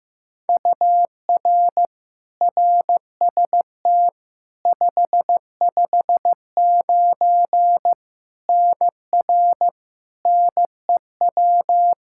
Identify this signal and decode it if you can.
Morse code, heard as a keyed tone, about 15 wpm, 700 Hz. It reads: UR RST 559 NR NEW
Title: K1USN SST QSO Practice 15wpm